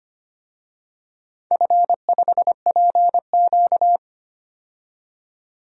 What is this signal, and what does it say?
Morse code, heard as a keyed tone, about 25 wpm, 700 Hz. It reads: F5PQ
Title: DXCC-100 Call Signs - 4 Characters - 25wpm